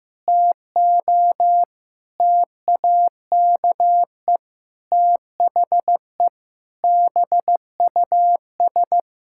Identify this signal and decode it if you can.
Morse code, heard as a keyed tone, about 16 wpm, 700 Hz. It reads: TO TAKE THE BUS
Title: Sentences from Top 1000 Words - Review Part 6 - 15wpm